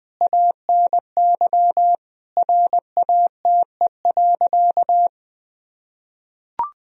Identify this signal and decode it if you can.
Morse code, heard as a keyed tone, about 19 wpm, 700 Hz. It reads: ANY RATE.
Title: Sentences from Top 1000 Words - Review Part 4 - 20wpm